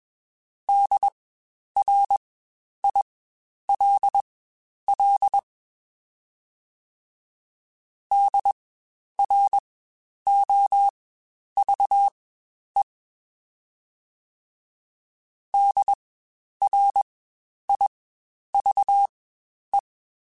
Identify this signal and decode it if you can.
Morse code, heard as a keyed tone, about 20 wpm, spaced as for 10 wpm, 800 Hz. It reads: DRILL DROVE DRIVE
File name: Spaced Alphas / Spaced Alphas mp3s / Ds sp @20